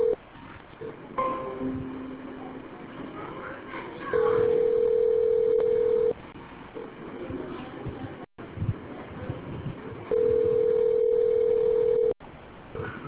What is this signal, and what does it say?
Ringback tone (North America), 440 + 480 Hz.